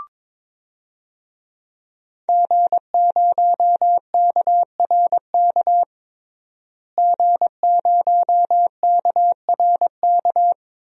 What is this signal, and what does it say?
Morse code, heard as a keyed tone, about 23 wpm, 700 Hz. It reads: G0KRK G0KRK
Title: DXCC-100 Call Signs - 5 Characters - Repeated 3x - 22wpm